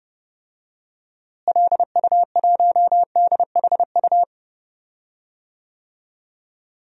Morse code, heard as a keyed tone, about 30 wpm, 700 Hz. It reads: LU1DHU
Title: DXCC-100 Call Signs - 6 Characters - Repeated 2x - 30wpm